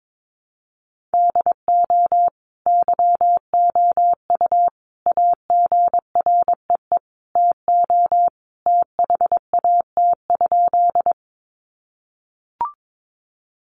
Morse code, heard as a keyed tone, about 22 wpm, 700 Hz. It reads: DO YOU AGREE TO THAT?
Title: Sentences from Top 1100 Words 22wpm